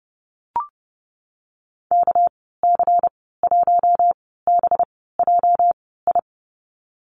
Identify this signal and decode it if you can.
Morse code, heard as a keyed tone, about 32 wpm, spaced as for 17 wpm, 700 Hz. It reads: KC1BJI
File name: Lesson 37g - Letter x - Call Signs 30wpm